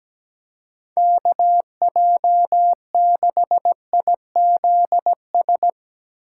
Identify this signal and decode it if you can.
Morse code, heard as a keyed tone, about 17 wpm, 700 Hz. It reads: KJ6IZS
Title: USA Call Signs - 6 Characters - 2x Character Spacing - 17wpm